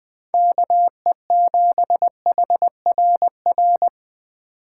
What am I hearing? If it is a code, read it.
Morse code, heard as a keyed tone, about 20 wpm, 700 Hz. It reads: KE7HRR
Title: Speed Racing - USA 2x3 Call Signs 17wpm